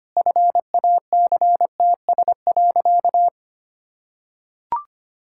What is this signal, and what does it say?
Morse code, heard as a keyed tone, about 25 wpm, 700 Hz. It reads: FACTS.
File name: practice / 25wpm / Sentences with 2-Word Phrases - 25wpm